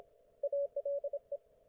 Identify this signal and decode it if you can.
Morse code, heard as a keyed tone, about 27 wpm, 570 Hz. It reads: ALE